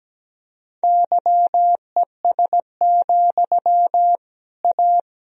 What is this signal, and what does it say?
Morse code, heard as a keyed tone, about 17 wpm, 700 Hz. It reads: YES, A
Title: Sentences from Top 1000 Words - Review Part 3 - 17wpm